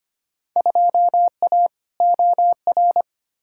Morse code, heard as a keyed tone, about 25 wpm, 700 Hz. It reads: 2A OR
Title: Field Day Exchanges Encore - 25wpm